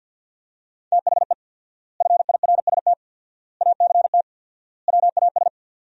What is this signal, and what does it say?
Morse code, heard as a keyed tone, about 50 wpm, 700 Hz. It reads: THE FIRST ACT WAS
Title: 4x Word-Spacing - Sentences from Top 200 Words - 50wpm